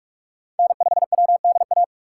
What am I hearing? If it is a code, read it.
Morse code, heard as a keyed tone, about 41 wpm, 700 Hz. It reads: N5WDA
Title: USA Call Signs - 6x Character Spacing - 45wpm